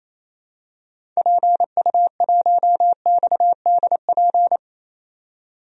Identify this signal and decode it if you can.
Morse code, heard as a keyed tone, about 28 wpm, 700 Hz. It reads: PU1XDP
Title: DXCC-100 Call Signs - 6 Characters - 28wpm